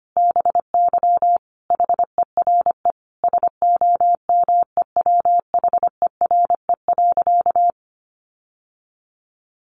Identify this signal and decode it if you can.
Morse code, heard as a keyed tone, about 25 wpm, 700 Hz. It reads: BY HERE SOMEWHERE.
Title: Sentences from Top 1400 Words 25wpm